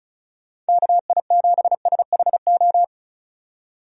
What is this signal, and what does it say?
Morse code, heard as a keyed tone, about 36 wpm, 700 Hz. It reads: KI7SHO
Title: USA Call Signs - 6 Characters - 35wpm